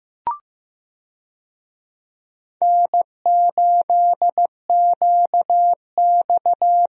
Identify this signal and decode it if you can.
Morse code, heard as a keyed tone, about 15 wpm, 700 Hz. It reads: N8QX
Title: USA Call Signs - 4 Characters - 15wpm